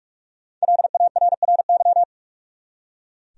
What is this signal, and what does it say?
Morse code, heard as a keyed tone, about 45 wpm, 700 Hz. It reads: LARRY
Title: USA Common Names - Max 5 Letters - 45wpm